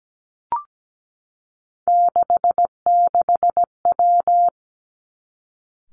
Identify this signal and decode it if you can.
Morse code, heard as a keyed tone, about 17 wpm, 700 Hz. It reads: 66W
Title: Three Challenging Letter-Number Combinations 17wpm